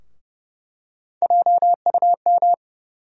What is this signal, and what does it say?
Morse code, heard as a keyed tone, about 30 wpm, 700 Hz. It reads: JUM
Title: USA Call Sign Postfixes - 3 Letters 30wpm